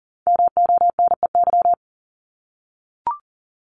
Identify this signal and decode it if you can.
Morse code, heard as a keyed tone, about 42 wpm, 700 Hz. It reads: MONEY
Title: Multi-letter Phonemes 40wpm